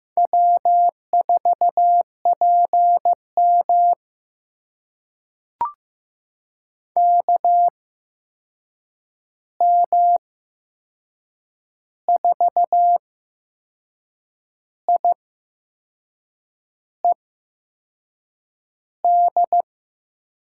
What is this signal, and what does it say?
Morse code, heard as a keyed tone, about 15 wpm, spaced as for 6 wpm, 700 Hz. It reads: W4PM KM4IED
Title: USA Call Signs - 8x Character Spacing - 15wpm